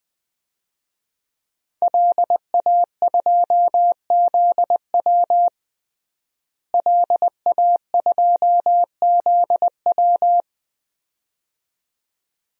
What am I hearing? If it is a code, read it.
Morse code, heard as a keyed tone, about 21 wpm, 700 Hz. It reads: LA2ZW LA2ZW
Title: DXCC-100 Call Signs - 5 Characters - Repeated 2x - 20wpm